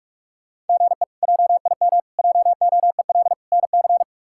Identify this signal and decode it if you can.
Morse code, heard as a keyed tone, about 44 wpm, 700 Hz. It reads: GE JIM JOEL NC
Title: K1USN SST QSO Practice - 45wpm